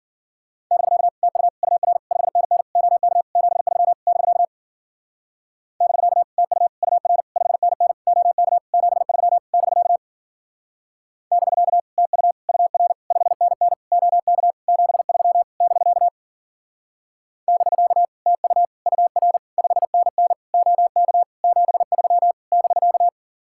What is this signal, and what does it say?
Morse code, heard as a keyed tone, about 52 wpm, 700 Hz. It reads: <BK> TU UR 5NN OK 73 <BK> <BK> TU UR 5NN OK 73 <BK> <BK> TU UR 5NN OK 73 <BK> <BK> TU UR 5NN OK 73 <BK>